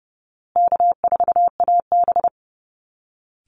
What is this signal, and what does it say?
Morse code, heard as a keyed tone, about 30 wpm, 700 Hz. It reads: K4AB
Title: Easy Call Signs Repeated 3x - 30wpm